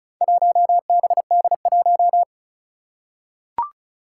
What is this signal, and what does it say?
Morse code, heard as a keyed tone, about 36 wpm, 700 Hz. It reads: 1BD1
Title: Four Challenging Letter-Number Combinations 35wpm